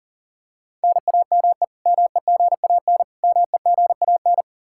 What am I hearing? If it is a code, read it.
Morse code, heard as a keyed tone, about 40 wpm, 700 Hz. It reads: NAME MEGAN MEGAN
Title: Practice QSO - Exchange 1 - Beginning - 40wpm